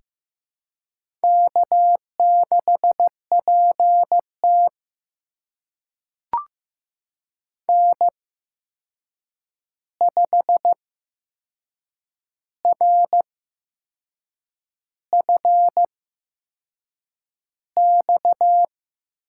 Morse code, heard as a keyed tone, about 15 wpm, spaced as for 6 wpm, 700 Hz. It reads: K6PT N5RFX
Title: USA Call Signs - 8x Character Spacing - 15wpm